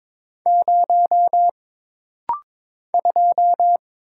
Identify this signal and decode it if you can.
Morse code, heard as a keyed tone, about 22 wpm, 700 Hz. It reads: 0 2
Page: This rapid-fire timing is geared to push your ICR (Instant Character Recognition) skills.